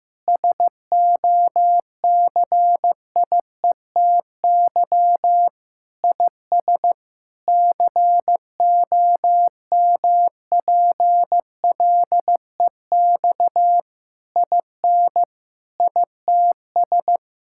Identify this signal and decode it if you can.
Morse code, heard as a keyed tone, about 16 wpm, 700 Hz. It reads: SOCIETY IS COMPLEX IN ITS
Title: Sentences from Top 1000 Words - Review Part 6 - 15wpm